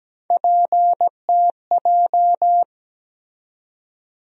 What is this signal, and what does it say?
Morse code, heard as a keyed tone, about 17 wpm, 700 Hz. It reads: PTJ
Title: USA Call Sign Postfixes - 3 Letters 17wpm